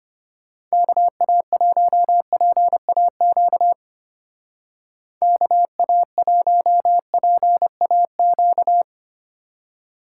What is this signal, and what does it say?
Morse code, heard as a keyed tone, about 30 wpm, 700 Hz. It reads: KA1PAQ KA1PAQ
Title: Speed Racing - USA 2x3 Call Signs 25wpm